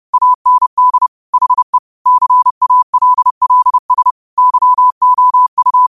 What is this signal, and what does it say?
Morse code, heard as a keyed tone, about 30 wpm, 1000 Hz. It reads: AND HE CALLS YOU
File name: qotdfiles / 30wpm